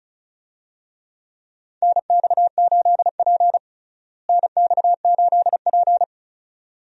Morse code, heard as a keyed tone, about 36 wpm, 700 Hz. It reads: NX8P NX8P
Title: USA Call Signs - 4 Characters - Repeated 2x - 35wpm